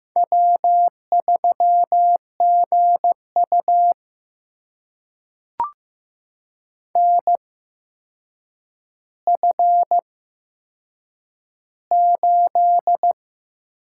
Morse code, heard as a keyed tone, about 15 wpm, spaced as for 7 wpm, 700 Hz. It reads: W3GU NF8
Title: USA Call Signs - 4 Characters - 8x Character Spacing - 15wpm